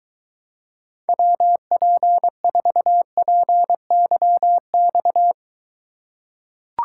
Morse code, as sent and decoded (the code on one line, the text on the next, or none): .-- .--. ....- .--. -.-- -..-
WP4PYX